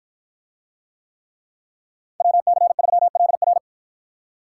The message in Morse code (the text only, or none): WC3LR